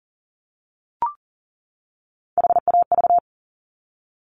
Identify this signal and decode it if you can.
Morse code, heard as a keyed tone, about 38 wpm, 690 Hz. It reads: HAV
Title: Top Three Letter Combinations in Words 40wpm